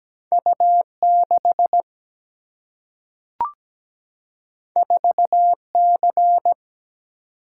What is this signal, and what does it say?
Morse code, heard as a keyed tone, about 17 wpm, 700 Hz. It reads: U6 4C
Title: Single-Letter Single-Number 17wpm